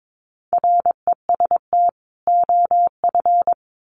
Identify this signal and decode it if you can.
Morse code, heard as a keyed tone, about 22 wpm, 700 Hz. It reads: REST OF
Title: Sets of 2 Words 22wpm